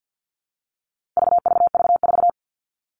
Morse code, heard as a keyed tone, about 50 wpm, 700 Hz. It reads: VVV4
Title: All Cut Numbers - 4-Digit 50wpm